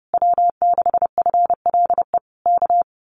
Morse code, heard as a keyed tone, about 30 wpm, 700 Hz. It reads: W6FLE K
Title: Common QSO Phrases Encore 30wpm